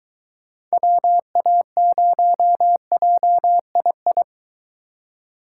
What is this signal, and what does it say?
Morse code, heard as a keyed tone, about 23 wpm, 700 Hz. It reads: WA0JII